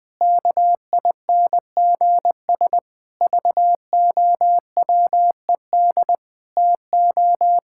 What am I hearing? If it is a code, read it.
Morse code, heard as a keyed tone, about 20 wpm, 700 Hz. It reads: KINGS VOWED TO
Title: Pangram Challenge v2 - 20wpm